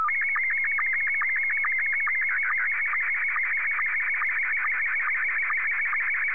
SSTV (slow-scan television), Scottie 1.